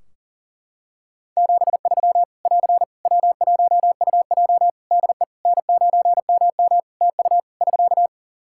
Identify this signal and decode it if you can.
Morse code, heard as a keyed tone, about 40 wpm, 700 Hz. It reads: 73 <AR> W1UJ DE N9MM TU <SK>
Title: Practice QSO - Exchange 4 - Ending - 40wpm